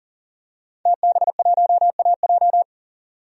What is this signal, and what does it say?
Morse code, heard as a keyed tone, about 40 wpm, 700 Hz. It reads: TB1AJ
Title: DXCC-100 Call Signs - 5 Characters - 40wpm